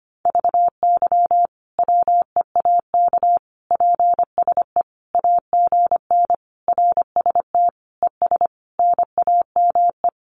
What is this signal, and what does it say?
Morse code, heard as a keyed tone, about 25 wpm, 700 Hz. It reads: VY WEAK PSE AGN RST ES NAME
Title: Common QSO Phrases 25wpm